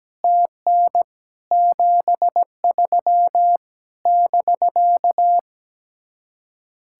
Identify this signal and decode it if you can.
Morse code, heard as a keyed tone, about 17 wpm, 700 Hz. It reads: TN 73 <BK>
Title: Speed-Racing Common QSO Phrases - 17wpm